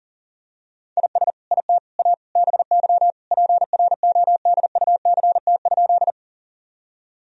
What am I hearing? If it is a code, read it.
Morse code, heard as a keyed tone, about 40 wpm, 700 Hz. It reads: IS IT A BY PRODUCT?
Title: Sentences from Top 1000 Words - Review Part 2 - 40wpm